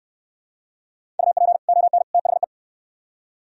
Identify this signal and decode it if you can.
Morse code, heard as a keyed tone, about 67 wpm, 700 Hz. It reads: UP ON THE